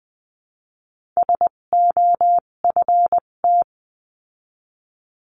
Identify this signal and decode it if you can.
Morse code, heard as a keyed tone, about 20 wpm, spaced as for 17 wpm, 700 Hz. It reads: SOFT
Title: 257 Lesson 42b - Words 14-20wpm